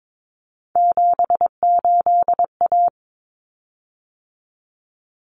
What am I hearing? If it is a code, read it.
Morse code, heard as a keyed tone, about 22 wpm, 700 Hz. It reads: 78A
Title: Common Cut Numbers - 3-Digit 22wpm